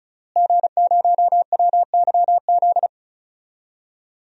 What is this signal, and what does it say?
Morse code, heard as a keyed tone, about 34 wpm, 700 Hz. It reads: G0WYZ
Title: DXCC-100 Call Signs - 5 Characters - Repeated 3x - 35wpm